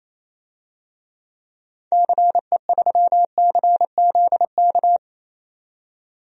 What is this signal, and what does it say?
Morse code, heard as a keyed tone, about 28 wpm, 700 Hz. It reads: CE3CZK
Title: DXCC-100 Call Signs - 6 Characters - 28wpm